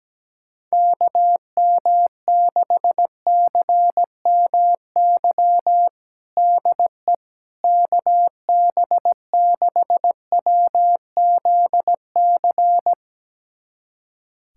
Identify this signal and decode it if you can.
Morse code, heard as a keyed tone, about 17 wpm, 700 Hz. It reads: KM6CMY DE KB6WZC
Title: USA Callsign DE Callsign - 6 Characters - 17wpm